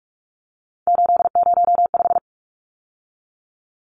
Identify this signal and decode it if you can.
Morse code, heard as a keyed tone, about 44 wpm, 700 Hz. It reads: 805
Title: Numbers - 3-Digit 45wpm